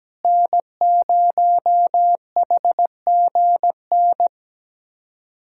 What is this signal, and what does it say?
Morse code, heard as a keyed tone, about 17 wpm, 700 Hz. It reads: N0HGN
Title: USA Callsign DE Callsign - 5 Characters - 17wpm